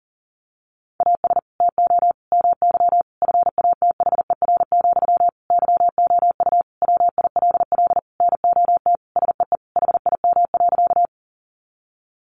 Morse code, heard as a keyed tone, about 40 wpm, 700 Hz. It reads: AS TO MY FATHER, YOU WILL NOT SEE HIM.
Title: Sentences from Top 600 Words 40wpm